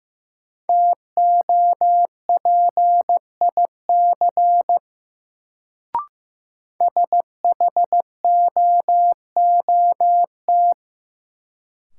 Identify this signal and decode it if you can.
Morse code, heard as a keyed tone, about 15 wpm, 700 Hz. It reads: TOPIC SHOOT
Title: Sentences from Top 1200 Words - New Word Drill Alt - 15wpm